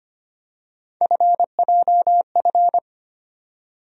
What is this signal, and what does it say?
Morse code, heard as a keyed tone, about 25 wpm, 700 Hz. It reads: FJF
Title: USA Call Sign Postfixes - 3 Letters 25wpm